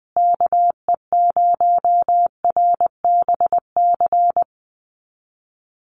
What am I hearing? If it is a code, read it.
Morse code, heard as a keyed tone, about 20 wpm, 700 Hz. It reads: KE0RBC
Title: USA Call Signs Repeated 2x 20wpm